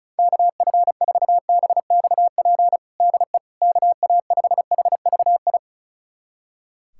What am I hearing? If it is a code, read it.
Morse code, heard as a keyed tone, about 36 wpm, 700 Hz. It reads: KF4BXP DE KA5HVI
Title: USA Callsign DE Callsign - 6 Characters - 35wpm